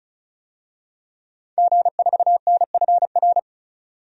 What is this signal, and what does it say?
Morse code, heard as a keyed tone, about 34 wpm, 700 Hz. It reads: G4NFR
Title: DXCC-100 Call Signs - 5 Characters - 35wpm